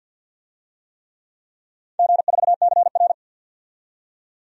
Morse code, heard as a keyed tone, about 52 wpm, 700 Hz. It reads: G4CR
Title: DXCC-100 Call Signs - 4 Characters - 50wpm